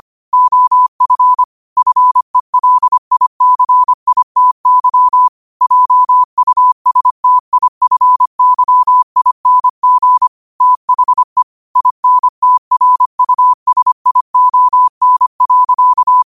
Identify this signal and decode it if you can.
Morse code, heard as a keyed tone, about 25 wpm, 1000 Hz. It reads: OF FELICITY JUSTIFYING THE INTRUSION.
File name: qotdfiles / 25wpm